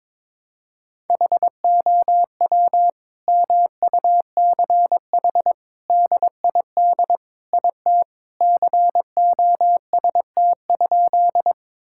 Morse code, heard as a keyed tone, about 22 wpm, 700 Hz. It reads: HOW MUCH DID IT COST?
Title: Sentences from Top 1000 Words - Review Part 5 - 22wpm